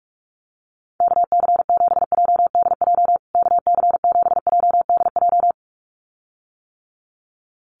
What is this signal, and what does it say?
Morse code, heard as a keyed tone, about 44 wpm, 700 Hz. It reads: KC7JDJ KC7JDJ
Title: USA Call Signs Repeated 2x 45wpm